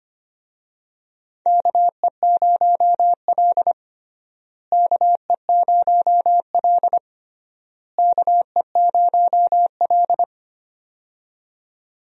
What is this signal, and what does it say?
Morse code, heard as a keyed tone, about 25 wpm, 700 Hz. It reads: KE0L KE0L KE0L
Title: USA Call Signs - 4 Characters - Repeated 3x - 25wpm